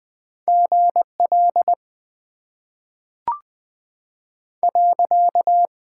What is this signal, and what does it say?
Morse code, heard as a keyed tone, about 20 wpm, 700 Hz. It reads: GL .
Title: Top 90 QSO Elements - 6x Character Spacing - 20wpm